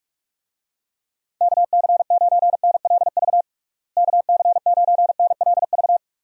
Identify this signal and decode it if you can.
Morse code, heard as a keyed tone, about 46 wpm, 700 Hz. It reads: KC9NLV KC9NLV
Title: USA Call Signs - 6 Characters - Repeated 3x - 45wpm